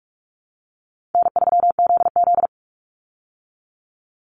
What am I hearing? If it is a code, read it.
Morse code, heard as a keyed tone, about 44 wpm, 700 Hz. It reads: N3ZZ
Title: Easy Call Signs Repeated 3x - 45wpm